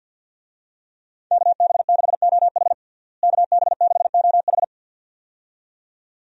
Morse code, heard as a keyed tone, about 47 wpm, 700 Hz. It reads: KB6OH KB6OH
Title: USA Call Signs - 5 Characters - Repeated 2x - 50wpm